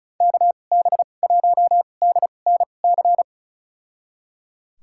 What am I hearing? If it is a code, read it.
Morse code, heard as a keyed tone, about 36 wpm, spaced as for 25 wpm, 700 Hz. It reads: KB1DNC
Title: USA Call Signs - 2x Character Spacing - 35wpm